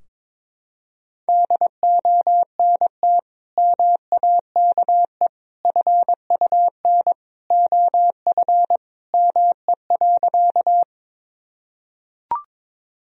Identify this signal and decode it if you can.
Morse code, heard as a keyed tone, about 22 wpm, 700 Hz. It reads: DONT MAKE FUN OF ME.
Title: Sentences from Top 1000 Words - Review Part 4 - 22wpm